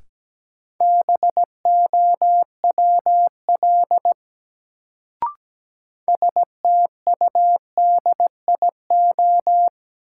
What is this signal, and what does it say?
Morse code, heard as a keyed tone, about 17 wpm, 700 Hz. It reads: BOWL STUDIO
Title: Sentences from Top 1200 Words - New Word Drill - 17wpm